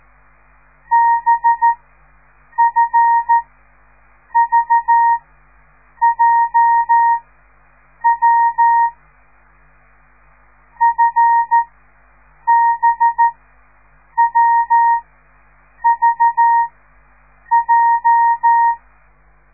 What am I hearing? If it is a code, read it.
Morse code, heard as a keyed tone, about 13 wpm, spaced as for 7 wpm, 930 Hz. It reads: BFVJW FBWVJ